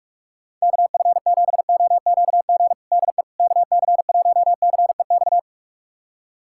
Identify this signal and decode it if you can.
Morse code, heard as a keyed tone, about 41 wpm, 690 Hz. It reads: KF7OQG DE KC1CEX
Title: USA Callsign DE Callsign - 6 Characters - 45wpm